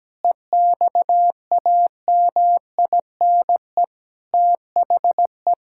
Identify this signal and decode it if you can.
Morse code, heard as a keyed tone, about 17 wpm, 700 Hz. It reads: EXAMINE THE
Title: Sentences from Top 1600 Words 17wpm